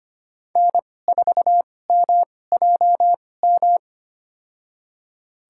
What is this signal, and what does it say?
Morse code, heard as a keyed tone, about 25 wpm, spaced as for 18 wpm, 700 Hz. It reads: N4MJM
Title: USA Call Signs - 2x Character Spacing - 25wpm